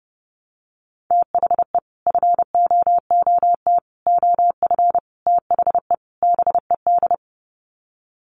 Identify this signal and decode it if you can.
Morse code, heard as a keyed tone, about 30 wpm, 700 Hz. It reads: THE FOOT OF THE BED
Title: Sets of 5 Words Encore2 30wpm